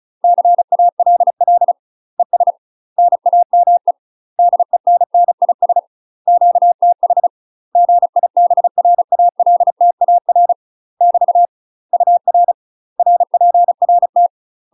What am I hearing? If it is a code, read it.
Morse code, heard as a keyed tone, about 34 wpm, 700 Hz. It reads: CALL ES NAME DENNIS QTH GIBRALTAR <BT> UR RPRT
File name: UK MASTER QSOs TRAINING / 35WPM / UK MASTER QSOs 0000